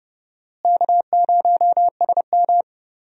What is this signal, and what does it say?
Morse code, heard as a keyed tone, about 32 wpm, 700 Hz. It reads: K0SM
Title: USA Call Signs - 4 Characters - Repeated 2x - 30wpm